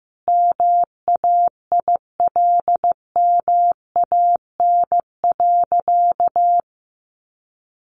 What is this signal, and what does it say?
Morse code, heard as a keyed tone, about 15 wpm, 700 Hz. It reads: MAILMAN.
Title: Easy Phrases 15wpm